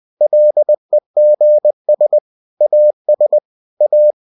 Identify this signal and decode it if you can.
Morse code, heard as a keyed tone, about 20 wpm, 600 Hz. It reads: LEGS AS A
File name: cw-books / winnie-the-pooh / winnie-20wpm-page-0019